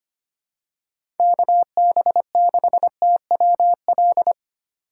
Morse code, heard as a keyed tone, about 25 wpm, 700 Hz. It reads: KB6TWL
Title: USA Call Signs - 2x Character Spacing - 25wpm